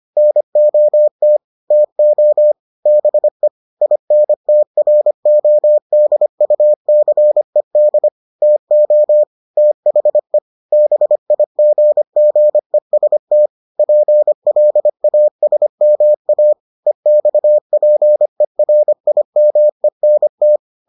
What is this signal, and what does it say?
Morse code, heard as a keyed tone, about 25 wpm, 600 Hz. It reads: NOT TO BE INTRODUCED TO THE BIGGEST PLASMA EXPERIMENT